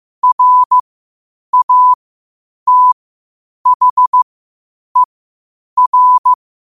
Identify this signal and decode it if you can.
Morse code, heard as a keyed tone, about 15 wpm, spaced as for 8 wpm, 1000 Hz. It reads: RATHER
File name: qotdfiles / 8wpm